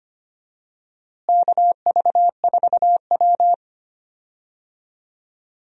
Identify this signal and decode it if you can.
Morse code, heard as a keyed tone, about 25 wpm, 700 Hz. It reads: KV4W
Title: USA Call Signs - 4 Characters - Repeated 3x - 25wpm